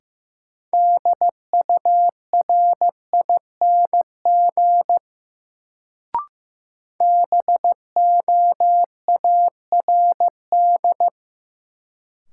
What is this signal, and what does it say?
Morse code, heard as a keyed tone, about 15 wpm, 700 Hz. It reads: DURING BOARD
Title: Sentences from Top 700 Words - New Word Drill - 15wpm